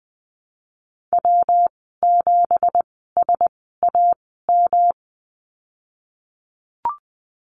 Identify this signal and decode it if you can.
Morse code, heard as a keyed tone, about 20 wpm, spaced as for 15 wpm, 700 Hz. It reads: W7SAM
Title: 179 Lesson 29g - Slash - Call Signs 20wpm